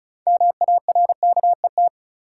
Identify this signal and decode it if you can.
Morse code, heard as a keyed tone, about 36 wpm, 700 Hz. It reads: MARKET
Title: Sentences from Top 600 Words - New Word Drill - 35wpm